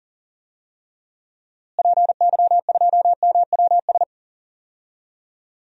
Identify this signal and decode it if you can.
Morse code, heard as a keyed tone, about 37 wpm, 700 Hz. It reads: PY2MWS
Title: DXCC-100 Call Signs - 6 Characters - Repeated 2x - 40wpm